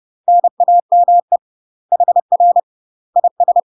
Morse code, heard as a keyed tone, about 30 wpm, 700 Hz. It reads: NAME HR IS
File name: UK MASTER QSOs TRAINING / 30WPM / UK MASTER QSOs 0015